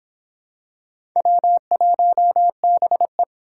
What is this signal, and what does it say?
Morse code, heard as a keyed tone, about 25 wpm, 700 Hz. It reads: W1BE